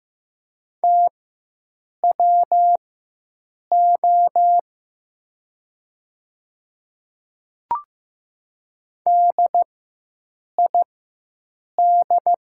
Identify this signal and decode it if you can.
Morse code, heard as a keyed tone, about 16 wpm, spaced as for 7 wpm, 700 Hz. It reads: TWO DID
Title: 4x Character Spacing - Top 300 Words - 15wpm